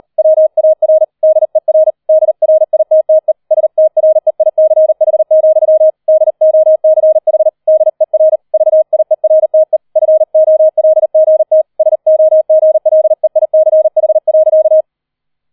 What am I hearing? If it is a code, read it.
Morse code, heard as a keyed tone, about 38 wpm, 630 Hz. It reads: WAR DER DRITTE STREICH, DOCH DER VIERTE FOLGT SOGLEICH.